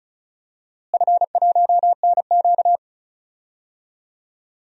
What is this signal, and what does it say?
Morse code, heard as a keyed tone, about 36 wpm, 700 Hz. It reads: F1NQ
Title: DXCC-100 Call Signs - 4 Characters - Repeated 2x - 35wpm